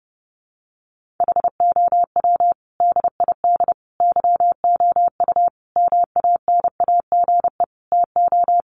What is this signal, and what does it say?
Morse code, heard as a keyed tone, about 30 wpm, 700 Hz. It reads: HOW DID YOU MANAGE TO
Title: Sentences from Top 1400 Words 30wpm